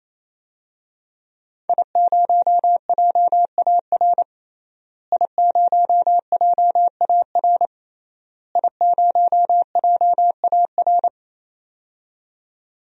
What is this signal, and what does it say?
Morse code, heard as a keyed tone, about 28 wpm, 690 Hz. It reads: I0JAR I0JAR I0JAR
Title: DXCC-100 Call Signs - 5 Characters - Repeated 3x - 28wpm